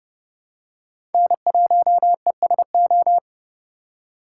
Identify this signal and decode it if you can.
Morse code, heard as a keyed tone, about 30 wpm, 700 Hz. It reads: N1ESO
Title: USA Call Signs - 4x Character Spacing - 30wpm